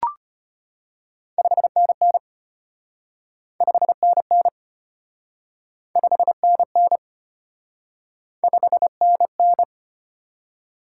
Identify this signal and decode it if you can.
Morse code, heard as a keyed tone, about 32 wpm, 700 Hz. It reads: 5NN 5NN 5NN 5NN